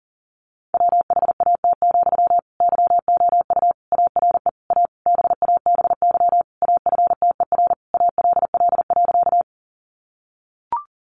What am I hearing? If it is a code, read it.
Morse code, heard as a keyed tone, about 37 wpm, 700 Hz. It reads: WHAT, YOU ARE A BABY AFTER ALL.
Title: Sentences from Top 1300 Words 40wpm